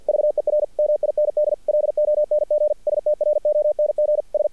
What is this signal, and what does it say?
Morse code, heard as a keyed tone, about 50 wpm, 600 Hz. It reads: VER MIND LONG STRONG S
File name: archived / CRSnet